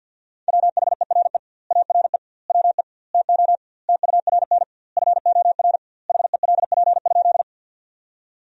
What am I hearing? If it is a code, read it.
Morse code, heard as a keyed tone, about 53 wpm, 700 Hz. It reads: WHERE ARE WE TO TURN FOR HELP?
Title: 2x Word-Spacing - Sentences from Top 200 Words - 50wpm